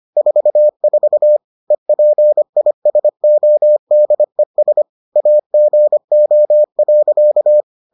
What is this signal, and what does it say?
Morse code, heard as a keyed tone, about 25 wpm, 600 Hz. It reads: 44 EPISODES AGO.